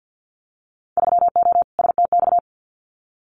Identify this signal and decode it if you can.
Morse code, heard as a keyed tone, about 50 wpm, 700 Hz. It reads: 3O STX
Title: Winter Field Day Exchanges 50wpm